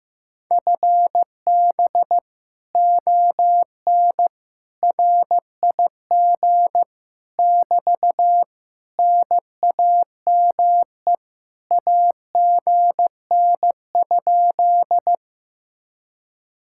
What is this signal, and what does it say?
Morse code, heard as a keyed tone, about 15 wpm, 700 Hz. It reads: FB ON RIG <BT> NAME AGN?
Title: Speed-Racing Common QSO Phrases Encore - 15wpm